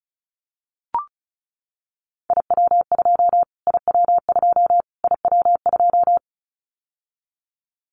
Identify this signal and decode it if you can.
Morse code, heard as a keyed tone, about 34 wpm, 700 Hz. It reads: IW2 IW2 IW2
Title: Callsign Prefixes for Top 25 DXCC - Repeated 3x - 35wpm